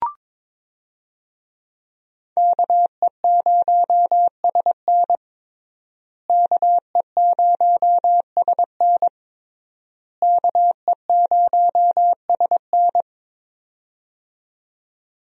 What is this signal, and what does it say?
Morse code, heard as a keyed tone, about 22 wpm, 700 Hz. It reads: KE0SN KE0SN KE0SN
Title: USA Call Signs - 5 Characters - Repeated 3x - 22wpm